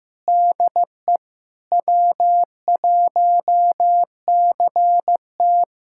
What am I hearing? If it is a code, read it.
Morse code, heard as a keyed tone, about 16 wpm, 700 Hz. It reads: DE W1CT
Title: USA Callsign DE Callsign - 4 Characters - 15wpm